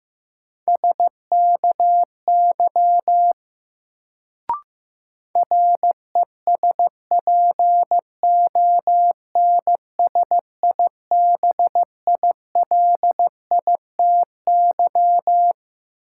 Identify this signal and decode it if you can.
Morse code, heard as a keyed tone, about 16 wpm, 700 Hz. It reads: SKY RESPONSIBILITY
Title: Sentences from Top 1200 Words - New Word Drill Alt - 15wpm